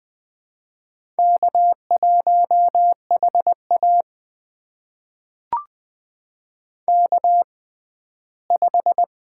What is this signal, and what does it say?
Morse code, heard as a keyed tone, about 20 wpm, spaced as for 14 wpm, 700 Hz. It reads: K1HA K5
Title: USA Call Signs - 4 Characters - 6x Character Spacing - 20wpm